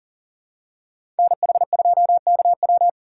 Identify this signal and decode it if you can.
Morse code, heard as a keyed tone, about 40 wpm, 700 Hz. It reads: NH2KW
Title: USA Call Signs - 2x Character Spacing - 40wpm